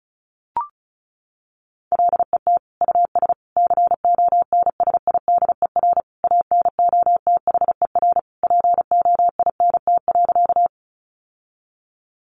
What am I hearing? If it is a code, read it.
Morse code, heard as a keyed tone, about 34 wpm, 700 Hz. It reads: LET US CONSIDER ANOTHER POINT.
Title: Sentences from Top 600 Words 35wpm